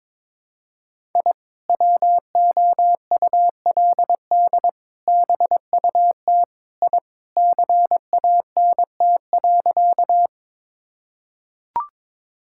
Morse code, heard as a keyed tone, about 22 wpm, 700 Hz. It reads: I WOULD BUT I CANT.
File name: practice / 22wpm / Sentences with 2-Word Phrases - 22wpm